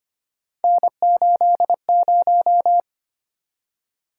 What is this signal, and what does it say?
Morse code, heard as a keyed tone, about 25 wpm, 700 Hz. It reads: N80
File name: Three Letter-Number Combinations - 25wpm